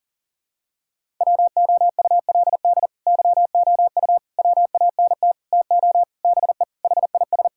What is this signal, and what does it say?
Morse code, heard as a keyed tone, about 40 wpm, 700 Hz. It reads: WOULD YOU WANT TO BE HIS
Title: Sentences from Top 1600 Words 40wpm